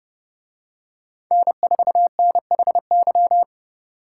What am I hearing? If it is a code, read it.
Morse code, heard as a keyed tone, about 30 wpm, 700 Hz. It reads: N4NHY